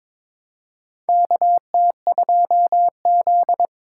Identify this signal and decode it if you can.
Morse code, heard as a keyed tone, about 21 wpm, 700 Hz. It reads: KT2Z
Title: USA Call Signs - 4 Characters - 2x Character Spacing - 22wpm